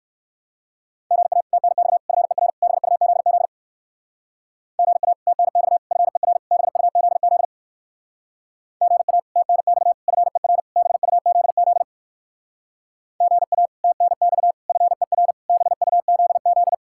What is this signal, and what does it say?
Morse code, heard as a keyed tone, about 58 wpm, 700 Hz. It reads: GA TNX FER BUZZ GA TNX FER BUZZ GA TNX FER BUZZ GA TNX FER BUZZ